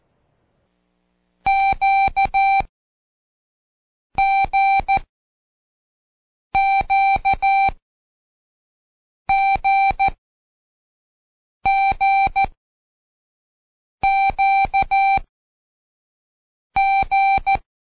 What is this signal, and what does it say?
Morse code, heard as a keyed tone, about 14 wpm, spaced as for 5 wpm, 780 Hz. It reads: QGQGGQG
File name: Code Quick / AudioDownld1 / Seg52-Day6-Pract Q